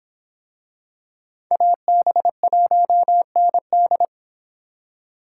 Morse code, heard as a keyed tone, about 26 wpm, 700 Hz. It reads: AB1ND